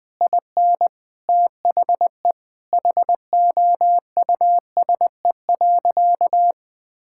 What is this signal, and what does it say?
Morse code, heard as a keyed tone, about 21 wpm, 700 Hz. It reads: IN THE HOUSE.
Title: Sentences from Top 1600 Words 20wpm